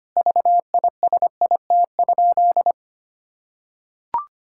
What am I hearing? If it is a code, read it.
Morse code, heard as a keyed tone, about 25 wpm, 700 Hz. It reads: VISIT?
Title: Sentences from Top 1000 Words - Review Part 6 - 25wpm